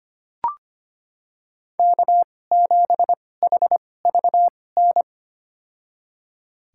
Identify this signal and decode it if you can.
Morse code, heard as a keyed tone, about 25 wpm, spaced as for 18 wpm, 700 Hz. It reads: K7HVN
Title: USA Call Signs - 5 Characters - 2x Character Spacing - 25wpm